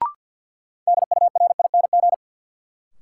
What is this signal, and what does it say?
Morse code, heard as a keyed tone, about 53 wpm, 700 Hz. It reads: DURING
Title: Sets of 2 Words - New Word Drill - 50wpm